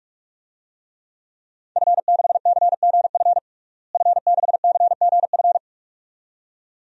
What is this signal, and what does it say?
Morse code, heard as a keyed tone, about 46 wpm, 700 Hz. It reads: F6CGF F6CGF
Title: DXCC-100 Call Signs - 5 Characters - Repeated 2x - 45wpm